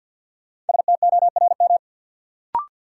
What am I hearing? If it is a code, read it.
Morse code, heard as a keyed tone, about 51 wpm, 700 Hz. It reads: STORM
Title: Sentences from Top 1500 Words - New Word Drill Alt - 50wpm